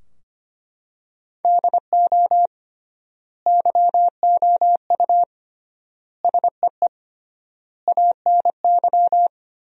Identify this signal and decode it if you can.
Morse code, heard as a keyed tone, about 25 wpm, 700 Hz. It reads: DO YOU SEE ANY
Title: 3x Word-Spacing - Sentences from Top 500 Words - 25wpm